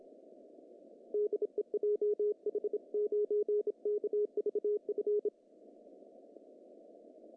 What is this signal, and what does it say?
Morse code, heard as a keyed tone, about 26 wpm, 1180 Hz. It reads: DE JH9KVF
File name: CW / cqww-cw-k5tr